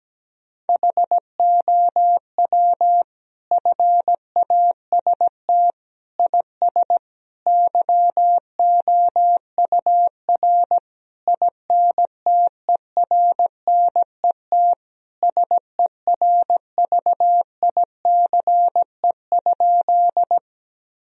Morse code, heard as a keyed tone, about 17 wpm, 700 Hz. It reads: HOW FAST IS YOUR INTERNET SERVICE?
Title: Sentences from Top 1000 Words - Review Part 4 - 17wpm